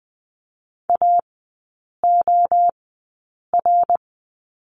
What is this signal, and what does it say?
Morse code, heard as a keyed tone, about 20 wpm, spaced as for 8 wpm, 700 Hz. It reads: AOR
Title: 093 Lesson 16e - Letter c - Three Characters 20wpm